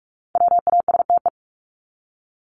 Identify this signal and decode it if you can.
Morse code, heard as a keyed tone, about 44 wpm, 700 Hz. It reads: WASTE
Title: Top 1751-2000 Words 45wpm